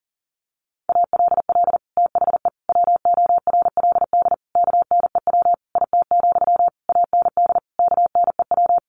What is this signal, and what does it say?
Morse code, heard as a keyed tone, about 40 wpm, 700 Hz. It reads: ALL THE WORLD KNEW IT, AND KNEW
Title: Sentences from Top 1200 Words 40wpm